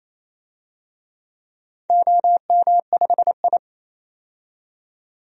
Morse code, heard as a keyed tone, about 28 wpm, 700 Hz. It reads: OM5I
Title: DXCC-100 Call Signs - 4 Characters - 28wpm